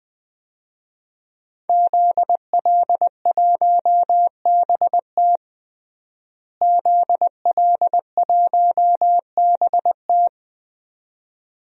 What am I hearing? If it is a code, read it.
Morse code, heard as a keyed tone, about 21 wpm, 700 Hz. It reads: ZL1BT ZL1BT
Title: DXCC-100 Call Signs - 5 Characters - Repeated 2x - 20wpm